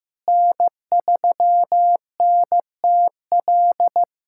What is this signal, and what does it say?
Morse code, heard as a keyed tone, about 16 wpm, 700 Hz. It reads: N3NTL
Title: USA Callsign DE Callsign - 5 Characters - 15wpm